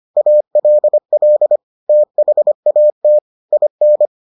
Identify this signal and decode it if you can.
Morse code, heard as a keyed tone, about 25 wpm, 600 Hz. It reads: ALL THAT IN